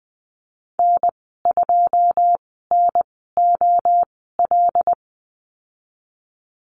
Morse code, heard as a keyed tone, about 20 wpm, spaced as for 14 wpm, 700 Hz. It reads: N2NOL
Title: 082 Lesson 14g - Number 2 - Call Signs 20wpm